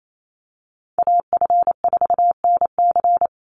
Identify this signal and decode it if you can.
Morse code, heard as a keyed tone, about 28 wpm, 700 Hz. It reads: AF4NC
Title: Call Signs Repeated 3x - 28wpm